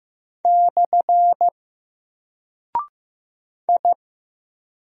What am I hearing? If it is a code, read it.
Morse code, heard as a keyed tone, about 15 wpm, 700 Hz. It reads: / I
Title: Single Character (Letter:Number:Punctuation:Prosign) Alt 15wpm